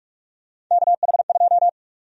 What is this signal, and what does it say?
Morse code, heard as a keyed tone, about 46 wpm, 700 Hz. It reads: KH2
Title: USA Call Sign Prefixes - Repeated 2x - 45wpm